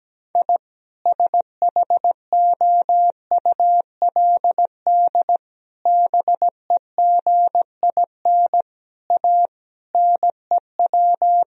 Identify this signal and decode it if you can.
Morse code, heard as a keyed tone, about 17 wpm, 700 Hz. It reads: I SHOULD BEGIN A NEW
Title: Sentences from Top 1000 Words - Review Part 1 - 17wpm